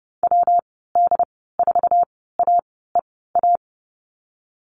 Morse code, heard as a keyed tone, about 30 wpm, spaced as for 17 wpm, 700 Hz. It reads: WD4AEA
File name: Lesson 19g - Letter w - Call Signs 30wpm